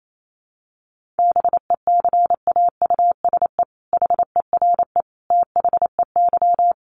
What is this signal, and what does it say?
Morse code, heard as a keyed tone, about 28 wpm, 700 Hz. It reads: BECAUSE HERE THEY
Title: Sentences from Top 1500 Words 28wpm